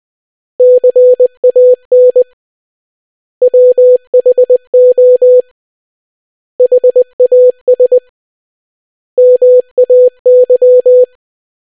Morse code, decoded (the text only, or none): CAN WHO HAS MAY